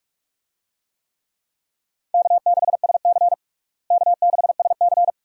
Morse code, heard as a keyed tone, about 46 wpm, 700 Hz. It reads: K6SC K6SC
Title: USA Call Signs - 4 Characters - Repeated 2x - 45wpm